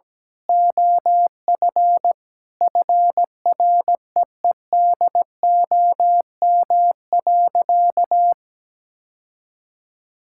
Morse code, heard as a keyed tone, about 17 wpm, 700 Hz. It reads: OF FREEDOM.
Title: Sentences from Top 1600 Words 17wpm